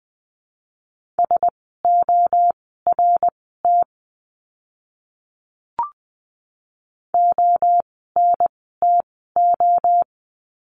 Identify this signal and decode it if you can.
Morse code, heard as a keyed tone, about 20 wpm, spaced as for 14 wpm, 700 Hz. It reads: SORT ONTO
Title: 054 Lesson 10f - Letter r - Words 20wpm